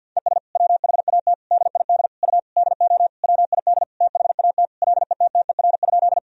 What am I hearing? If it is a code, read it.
Morse code, heard as a keyed tone, about 46 wpm, 700 Hz. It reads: ES WHAT DID U DO WID THAT LETTER?
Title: Sentences from Top 300 Words and Abbreviations 50wpm